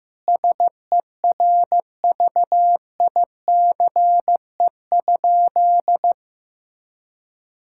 Morse code, heard as a keyed tone, about 16 wpm, 700 Hz. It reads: SERVICE?
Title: Sentences from Top 1000 Words - Review Part 6 - 15wpm